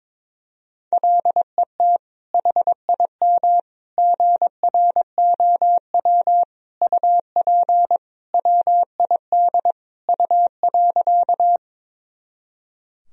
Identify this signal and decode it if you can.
Morse code, heard as a keyed tone, about 22 wpm, 700 Hz. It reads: LET HIM GROW UP WID U.
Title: Sentences from Top 300 Words and Abbreviations 22wpm